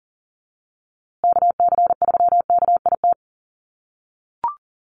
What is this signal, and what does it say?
Morse code, heard as a keyed tone, about 37 wpm, 700 Hz. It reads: KC3KIT
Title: USA Call Signs Repeated 3x 40wpm